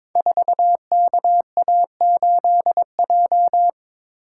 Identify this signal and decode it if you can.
Morse code, heard as a keyed tone, about 22 wpm, 700 Hz. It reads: 4KA8J
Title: Code Groups - 5 Letter:Numbers - 22wpm